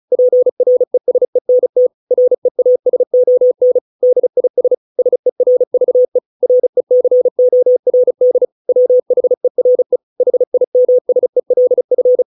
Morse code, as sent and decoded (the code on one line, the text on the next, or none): .--. .-. ... -. - / .-. .- ... --- -. / -.. .. ... / ... .-. ...- . / .-. -.-. --- .-. -.. / .-- .... .-. / .... .. -- ... .-.. ..-.
PRESENT REASON DIS SERVE RECORD WHERE HIMSELF